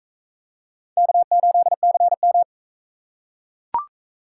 Morse code, heard as a keyed tone, about 40 wpm, 700 Hz. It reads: K8CM